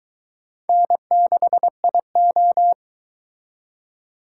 Morse code, heard as a keyed tone, about 22 wpm, 700 Hz. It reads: N6IO